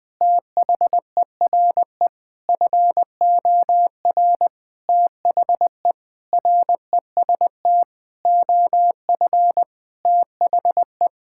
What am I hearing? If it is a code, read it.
Morse code, heard as a keyed tone, about 20 wpm, 700 Hz. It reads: THERE FOR THE REST OF THE
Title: Sentences with 5-Word Phrases - Spoken Beginning - 20wpm